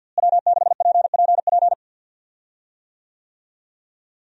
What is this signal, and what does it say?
Morse code, heard as a keyed tone, about 51 wpm, 700 Hz. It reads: W6PPP
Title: USA Call Signs - 5 Characters - Repeated 2x - 50wpm